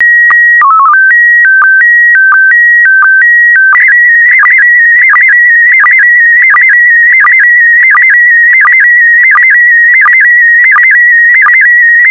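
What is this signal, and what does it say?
SSTV (slow-scan television), PD 90.
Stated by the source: SSTV image transmitted in PD90 mode.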